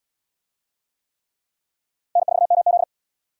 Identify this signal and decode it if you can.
Morse code, heard as a keyed tone, about 67 wpm, 700 Hz. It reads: N4MZ